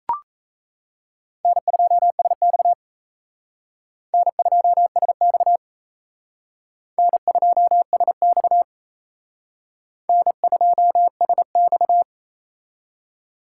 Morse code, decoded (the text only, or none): N2SX N2SX N2SX N2SX